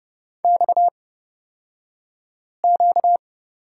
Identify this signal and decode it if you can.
Morse code, heard as a keyed tone, about 29 wpm, spaced as for 5 wpm, 700 Hz. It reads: XQ
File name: Lesson 38d - Letter q - Two Characters 30wpm